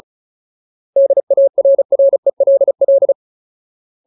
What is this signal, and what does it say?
Morse code, heard as a keyed tone, about 34 wpm, 550 Hz. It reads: DARRELL
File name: Ultimate Word List / 35WPM Ultimate